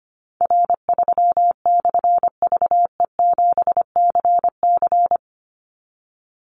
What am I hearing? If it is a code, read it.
Morse code, heard as a keyed tone, about 25 wpm, 700 Hz. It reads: R3/VE7CC
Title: Call Signs - 25wpm